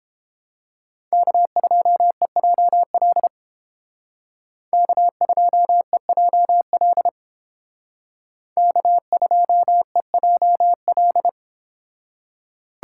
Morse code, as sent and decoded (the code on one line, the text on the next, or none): -.- ..--- . .--- .-.. / -.- ..--- . .--- .-.. / -.- ..--- . .--- .-..
K2EJL K2EJL K2EJL